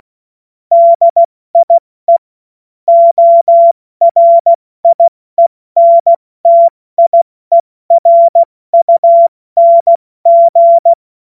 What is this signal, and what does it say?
Morse code, heard as a keyed tone, about 16 wpm, 680 Hz. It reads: DIE ORIENTIERUNG
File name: Morsekurs / Witze / Witz 16. August 70 F80